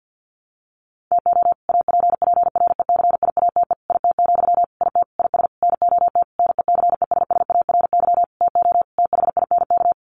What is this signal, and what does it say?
Morse code, heard as a keyed tone, about 50 wpm, 700 Hz. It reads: TO APPRECIATE IT, IT IS NOT NECESSARY TO THINK
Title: Sentences from Top 1300 Words 50wpm